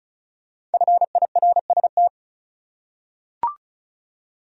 Morse code, heard as a keyed tone, about 36 wpm, 700 Hz. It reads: FIRST
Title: Top 100 Words 35wpm